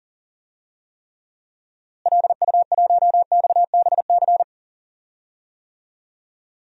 Morse code, heard as a keyed tone, about 40 wpm, 700 Hz. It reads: LU1XBC
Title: DXCC-100 Call Signs - 6 Characters - Repeated 3x - 40wpm